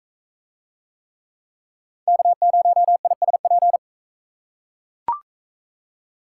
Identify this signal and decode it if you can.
Morse code, heard as a keyed tone, about 41 wpm, 700 Hz. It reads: K0ISP